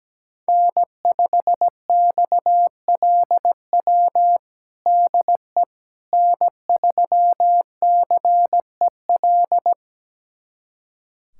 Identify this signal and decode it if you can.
Morse code, heard as a keyed tone, about 17 wpm, 700 Hz. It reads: N5XLW DE N3CEL
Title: USA Callsign DE Callsign 17wpm